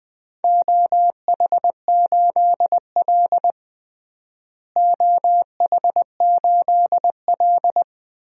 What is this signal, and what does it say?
Morse code, heard as a keyed tone, about 21 wpm, 700 Hz. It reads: OH8L OH8L
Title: DXCC-100 Call Signs - 4 Characters - Repeated 2x - 20wpm